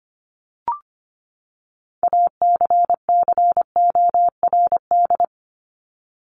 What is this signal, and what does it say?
Morse code, heard as a keyed tone, about 25 wpm, 700 Hz. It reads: ACCORD
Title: ING Suffix - Top 100 Words with Base Words 25wpm